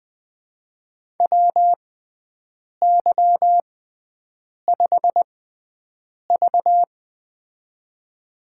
Morse code, heard as a keyed tone, about 21 wpm, spaced as for 7 wpm, 700 Hz. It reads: WY5V
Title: USA Call Signs - 4 Characters - 6x Character Spacing - 20wpm